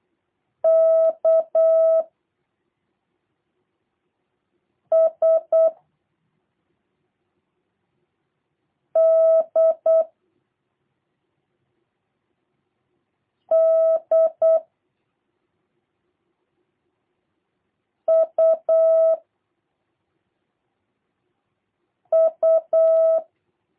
Morse code, heard as a keyed tone, about 8 wpm, 650 Hz. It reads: K S D D U U
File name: jh8cbh / 2023cw / review1-3